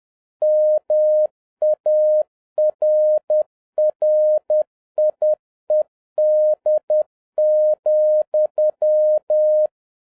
Morse code, heard as a keyed tone, about 10 wpm, 610 Hz. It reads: MARRIED,